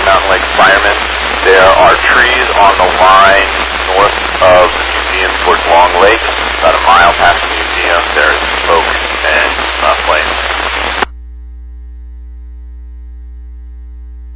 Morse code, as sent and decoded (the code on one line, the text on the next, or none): none